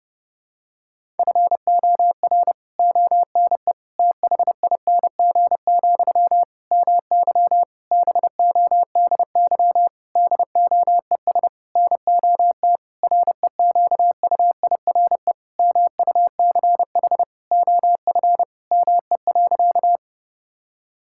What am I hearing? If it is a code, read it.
Morse code, heard as a keyed tone, about 30 wpm, 700 Hz. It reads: FOR ONE THING, MY BODY DOES NOT REQUIRE MUCH OF ME.
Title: Sentences from Top 1000 Words - Review Part 6 - 30wpm